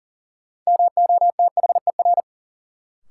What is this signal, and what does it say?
Morse code, heard as a keyed tone, about 40 wpm, 700 Hz. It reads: MOTHER
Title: Sentences from Top 200 Words - New Word Drill - 40wpm